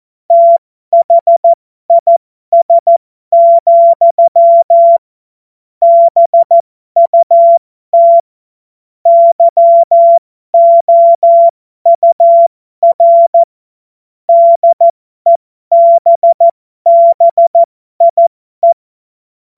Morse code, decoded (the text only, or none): THIS, BUT YOUR DEBBIE